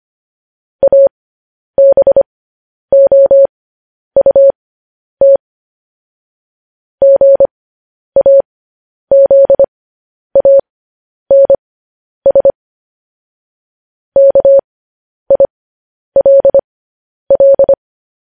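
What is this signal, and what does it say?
Morse code, heard as a keyed tone, about 25 wpm, spaced as for 10 wpm, 550 Hz. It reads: ABOUT GAZANS KILL
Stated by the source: Morse code transcription of news headlines sent at 10 wpm.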